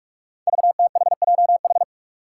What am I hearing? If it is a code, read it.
Morse code, heard as a keyed tone, about 44 wpm, 700 Hz. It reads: VTHJH
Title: Five Challenging Letter-Number Combinations 45wpm